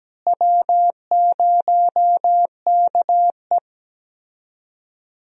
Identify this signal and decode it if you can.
Morse code, heard as a keyed tone, about 17 wpm, 700 Hz. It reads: W0KE
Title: USA Call Signs - 4 Characters - Repeated 2x - 17wpm